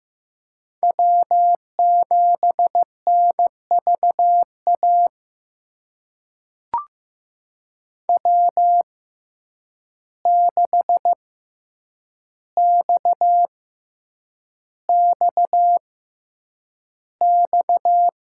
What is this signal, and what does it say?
Morse code, heard as a keyed tone, about 15 wpm, spaced as for 8 wpm, 700 Hz. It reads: W7NVA W6XXX
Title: USA Call Signs - 5 Characters - 6x Character Spacing - 15wpm